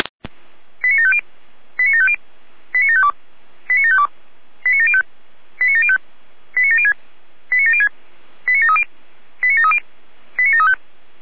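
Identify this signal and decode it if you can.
Five-tone selective call (ZVEI), heard as quick runs of five tones, tones of about 70 ms. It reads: 89750 89750 89752 89752 89705 89705 89706 89706 89730 89730 89735
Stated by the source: ZVEI five-tone selective call received on an SDR.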